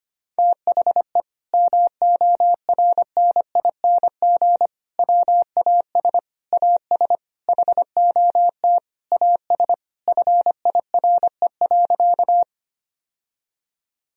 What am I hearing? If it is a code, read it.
Morse code, heard as a keyed tone, about 25 wpm, 700 Hz. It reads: THE MORNING WAS AS HOT AS FIRE.
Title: Sentences from Top 1000 Words - Review Part 3 - 25wpm